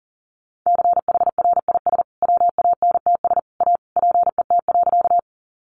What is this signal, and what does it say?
Morse code, heard as a keyed tone, about 40 wpm, 700 Hz. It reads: CHRIS WANTS A PET.
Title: Additional Easy Phrases 40wpm